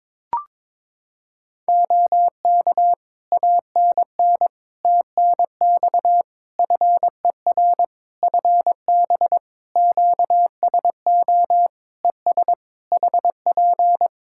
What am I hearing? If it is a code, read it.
Morse code, heard as a keyed tone, about 23 wpm, 700 Hz. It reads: OK ANN TNX FER FB QSO ES HP
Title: Practice QSO - Exchange 4 - Ending - 22wpm